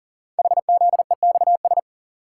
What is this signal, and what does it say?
Morse code, heard as a keyed tone, about 39 wpm, 690 Hz. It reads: HZEXS
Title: Code Groups - 5 Letters - 40wpm